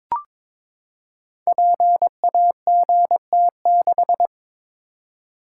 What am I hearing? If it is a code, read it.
Morse code, heard as a keyed tone, about 22 wpm, 700 Hz. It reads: PAGT6
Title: Code Groups - 5 Letter:Numbers - 22wpm